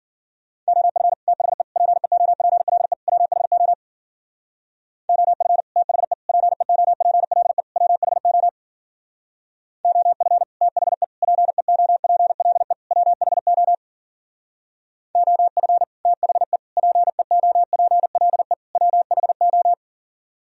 Morse code, heard as a keyed tone, about 52 wpm, 700 Hz. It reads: OF THE PEOPLE WHO OF THE PEOPLE WHO OF THE PEOPLE WHO OF THE PEOPLE WHO